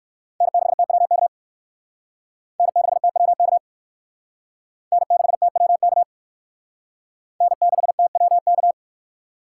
Morse code, heard as a keyed tone, about 57 wpm, 700 Hz. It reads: N6TWK N6TWK N6TWK N6TWK